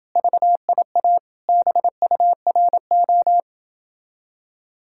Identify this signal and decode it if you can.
Morse code, heard as a keyed tone, about 29 wpm, 700 Hz. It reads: VIA BURO